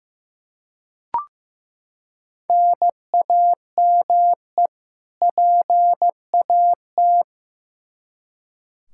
Morse code, heard as a keyed tone, about 15 wpm, 700 Hz. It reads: NAME PAT
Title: Practice QSO - Exchange 1 - Beginning - 15wpm